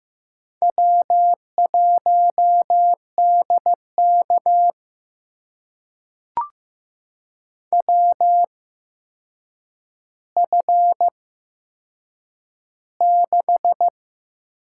Morse code, heard as a keyed tone, about 15 wpm, spaced as for 7 wpm, 700 Hz. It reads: W1DK WF6
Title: USA Call Signs - 4 Characters - 8x Character Spacing - 15wpm